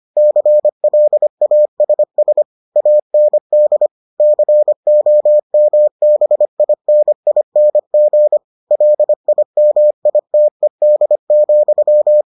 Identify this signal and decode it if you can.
Morse code, heard as a keyed tone, about 25 wpm, 600 Hz. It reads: CLASS AND COMBINING LIMITED,